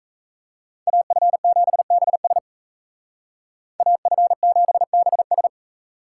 Morse code, heard as a keyed tone, about 35 wpm, 700 Hz. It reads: AF7BS AF7BS